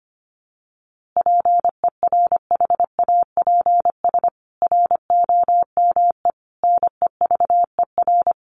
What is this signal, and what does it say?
Morse code, heard as a keyed tone, about 25 wpm, 700 Hz. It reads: PERHAPS ROME NEVER
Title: A Sound Of Thunder 25wpm